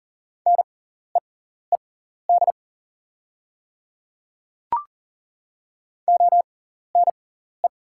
Morse code, heard as a keyed tone, about 37 wpm, spaced as for 14 wpm, 700 Hz. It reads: NEED ONE
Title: 6x Character Spacing - Top 200 Words - 40wpm